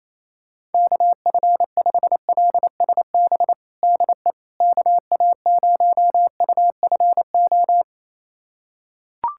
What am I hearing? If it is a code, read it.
Morse code, heard as a keyed tone, about 28 wpm, 700 Hz. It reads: KF5LSB DE KA0UFO
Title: USA Callsign DE Callsign 28wpm